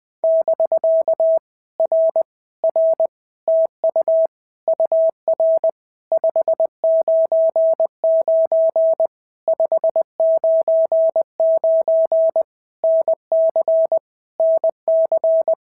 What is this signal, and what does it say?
Morse code, heard as a keyed tone, about 20 wpm, 650 Hz. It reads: <BK> R R TU UR 599 599 NC NC
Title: POTA QSO Practice - 20wpm